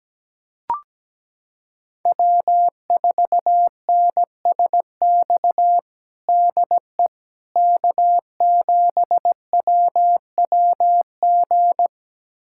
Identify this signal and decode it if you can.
Morse code, heard as a keyed tone, about 17 wpm, 700 Hz. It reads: W4NSX DE K7WWG
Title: USA Callsign DE Callsign 17wpm